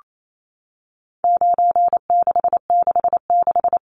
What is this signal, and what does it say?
Morse code, heard as a keyed tone, about 28 wpm, 700 Hz. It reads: 9666
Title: Numbers - 4-Digit 28wpm